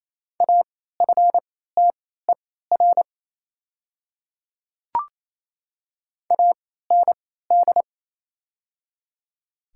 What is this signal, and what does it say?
Morse code, heard as a keyed tone, about 28 wpm, spaced as for 16 wpm, 700 Hz. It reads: AFTER AND
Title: 3x Character Spacing - Top 300 Words - 28wpm